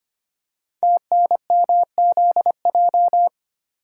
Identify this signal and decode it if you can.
Morse code, heard as a keyed tone, about 25 wpm, 700 Hz. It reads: TNMZJ
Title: Code Groups - 5 Letters - 25wpm